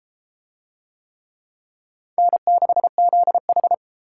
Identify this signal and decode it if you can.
Morse code, heard as a keyed tone, about 28 wpm, 700 Hz. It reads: N6ZH